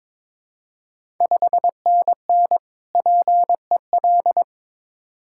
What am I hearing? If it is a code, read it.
Morse code, heard as a keyed tone, about 22 wpm, 690 Hz. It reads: 5NN PEL
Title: Ontario QSO Party Exchange 22wpm